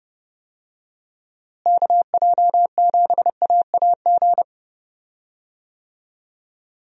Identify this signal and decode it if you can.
Morse code, heard as a keyed tone, about 28 wpm, 700 Hz. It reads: KJ7AAG
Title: USA Call Signs - 6 Characters - Repeated 2x - 30wpm